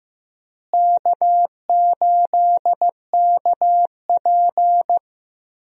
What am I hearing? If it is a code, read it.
Morse code, heard as a keyed tone, about 15 wpm, 700 Hz. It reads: K8KP
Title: USA Call Signs - 4 Characters - Repeated 3x - 15wpm